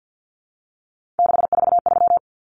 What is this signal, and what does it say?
Morse code, heard as a keyed tone, about 50 wpm, 700 Hz. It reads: 643
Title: Common Cut Numbers - 3-Digit 50wpm